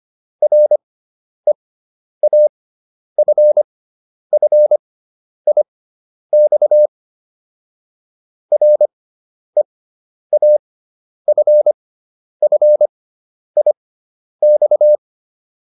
Morse code, heard as a keyed tone, about 25 wpm, spaced as for 10 wpm, 600 Hz. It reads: REAFFIX REAFFIX